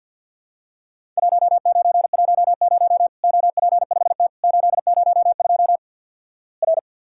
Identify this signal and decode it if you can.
Morse code, heard as a keyed tone, about 50 wpm, 700 Hz. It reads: 1910 OP5T 802 R
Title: Worked All Europe DX Contest - QTC Exchange with Repeats - 50wpm